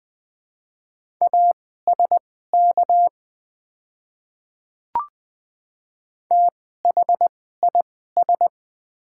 Morse code, heard as a keyed tone, about 20 wpm, spaced as for 14 wpm, 700 Hz. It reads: ASK THIS
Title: 2x Character Spacing - Top 200 Words - 20wpm